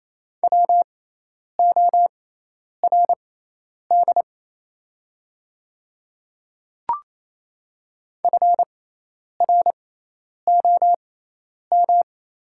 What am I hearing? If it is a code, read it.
Morse code, heard as a keyed tone, about 28 wpm, spaced as for 10 wpm, 700 Hz. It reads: WORD FROM
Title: 6x Character Spacing - Top 100 Words - 28wpm